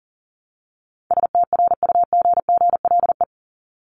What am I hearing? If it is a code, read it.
Morse code, heard as a keyed tone, about 39 wpm, 700 Hz. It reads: STRUGGLE
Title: Top 1751-2000 Words 40wpm